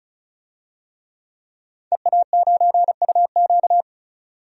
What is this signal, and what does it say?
Morse code, heard as a keyed tone, about 34 wpm, 700 Hz. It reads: EA9UQ
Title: DXCC-100 Call Signs - 5 Characters - 35wpm